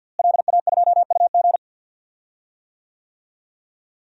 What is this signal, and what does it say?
Morse code, heard as a keyed tone, about 47 wpm, 700 Hz. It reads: LA2UG
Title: DXCC-100 Call Signs - 5 Characters - Repeated 2x - 50wpm